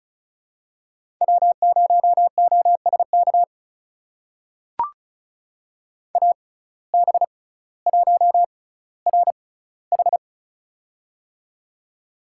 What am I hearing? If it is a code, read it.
Morse code, heard as a keyed tone, about 36 wpm, spaced as for 18 wpm, 700 Hz. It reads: W0OSK AB1RH
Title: USA Call Signs - 5 Characters - 6x Character Spacing - 35wpm